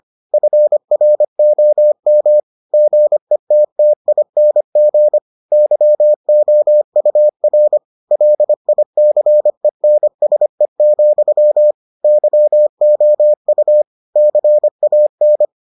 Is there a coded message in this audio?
Yes: Morse code from 0.3 to 15.4 s, heard as a keyed tone, about 25 wpm, 600 Hz. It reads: FROM GETTING YOUR LICENSE, YOU CAN